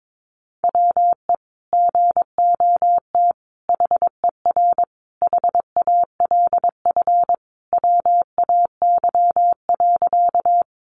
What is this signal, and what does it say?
Morse code, heard as a keyed tone, about 22 wpm, 700 Hz. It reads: WE GOT HER HALF WAY.
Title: Sentences from Top 400 Words 22wpm